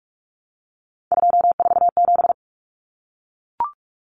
Morse code, heard as a keyed tone, about 44 wpm, 700 Hz. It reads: 247
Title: Numbers - 3-Digit 45wpm